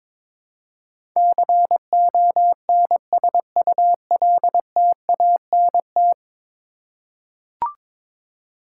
Morse code, heard as a keyed tone, about 22 wpm, 700 Hz. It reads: CONSULTANT
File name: practice / 22wpm / 10 Letter Words - First 3 Letters Given - 22wpm